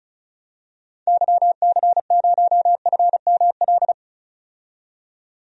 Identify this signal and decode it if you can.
Morse code, heard as a keyed tone, about 34 wpm, 700 Hz. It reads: YC0FML
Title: DXCC-100 Call Signs - 6 Characters - 35wpm